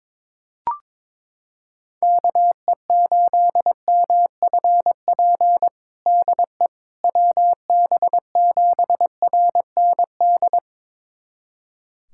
Morse code, heard as a keyed tone, about 22 wpm, 700 Hz. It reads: KE8MFP DE WB7RND
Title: USA Callsign DE Callsign - 6 Characters - 22wpm